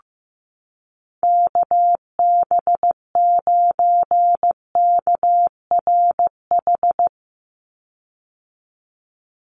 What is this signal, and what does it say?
Morse code, heard as a keyed tone, about 15 wpm, 700 Hz. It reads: KB9KRH
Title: USA Call Signs 15wpm